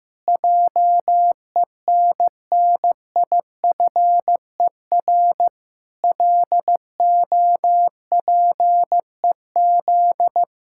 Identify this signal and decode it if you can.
Morse code, heard as a keyed tone, about 15 wpm, 700 Hz. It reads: JENNIFER LOPEZ
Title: Pangram Challenge v2 - 15wpm